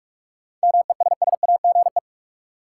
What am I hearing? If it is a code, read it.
Morse code, heard as a keyed tone, about 45 wpm, 700 Hz. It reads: MESSAGE
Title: Sentences from Top 1100 Words - New Word Drill Alt - 45wpm